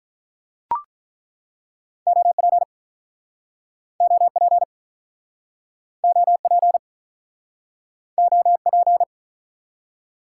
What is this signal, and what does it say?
Morse code, heard as a keyed tone, about 49 wpm, 700 Hz. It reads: OP OP OP OP